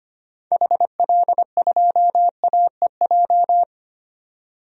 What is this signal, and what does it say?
Morse code, heard as a keyed tone, about 25 wpm, 700 Hz. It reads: HL2AEJ
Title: DXCC-100 Call Signs - 6 Characters - 25wpm